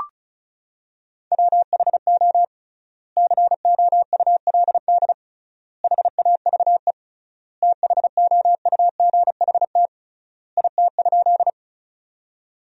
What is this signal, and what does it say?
Morse code, heard as a keyed tone, about 36 wpm, 700 Hz. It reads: WHO COULD HAVE THOUGHT IT?
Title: 3x Word-Spacing - Sentences from Top 300 Words - 35wpm